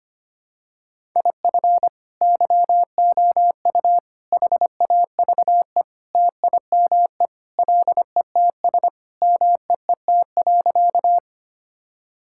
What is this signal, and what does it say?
Morse code, heard as a keyed tone, about 25 wpm, 700 Hz. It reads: IF YOU HAVE TIME LETS MEET.
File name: practice / 25wpm / Sentences with 3-Word Phrases - 25wpm